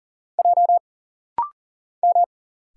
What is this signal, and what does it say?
Morse code, heard as a keyed tone, about 40 wpm, 700 Hz. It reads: J M
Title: Single Letter-Number - Rapid-Fire 40wpm